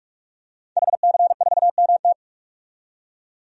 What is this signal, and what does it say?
Morse code, heard as a keyed tone, about 46 wpm, 700 Hz. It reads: HC4MT
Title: DXCC-100 Call Signs - 5 Characters - 45wpm